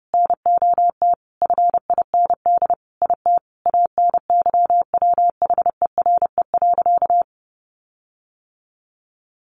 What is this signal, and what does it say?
Morse code, heard as a keyed tone, about 30 wpm, 700 Hz. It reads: NOT FIND IT ANYWHERE.
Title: Sentences from Top 1400 Words 30wpm